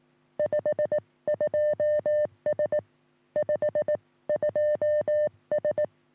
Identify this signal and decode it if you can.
Morse code, heard as a keyed tone, about 18 wpm, 590 Hz. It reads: 52S 52S
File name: sm7lcb / 1941 sm0err 10g rs